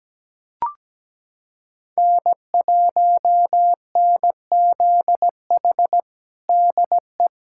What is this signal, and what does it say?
Morse code, heard as a keyed tone, about 17 wpm, 700 Hz. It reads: N1NZH DE
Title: USA Callsign DE Callsign - 5 Characters - 17wpm